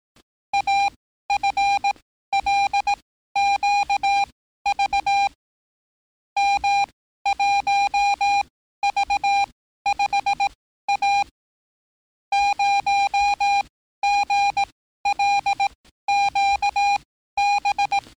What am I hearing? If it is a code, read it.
Morse code, heard as a keyed tone, about 18 wpm, spaced as for 13 wpm, 800 Hz. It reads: AFLQV M1V5A 0GLQB
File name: training / NAVPERS 11049to11060 / NAVPERS 11054 RA RB